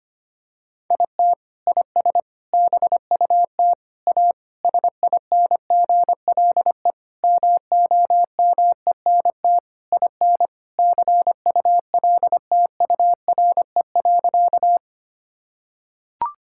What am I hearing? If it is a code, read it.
Morse code, heard as a keyed tone, about 25 wpm, 700 Hz. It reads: IT IS BUT A SINGLE MOMENT IN CULTURE.
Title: Sentences from Top 1000 Words - Review Part 6 - 25wpm